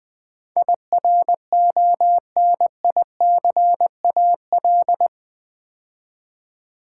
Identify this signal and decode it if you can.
Morse code, heard as a keyed tone, about 20 wpm, 700 Hz. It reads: IRONICAL
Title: Build A Word - 4 Letters At A Time - 10-Letter Words - 20wpm